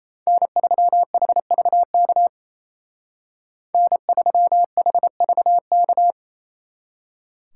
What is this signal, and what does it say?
Morse code, heard as a keyed tone, about 34 wpm, 700 Hz. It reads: N3HVK N3HVK
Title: Speed Racing - USA Call Signs 28wpm